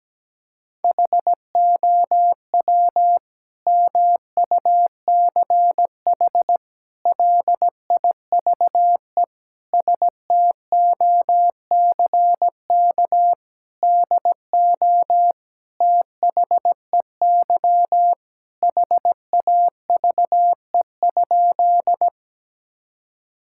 Morse code, heard as a keyed tone, about 17 wpm, 700 Hz. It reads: HOW MUCH LIVE STOCK DO THEY HAVE?
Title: Sentences from Top 1000 Words - Review Part 5 - 17wpm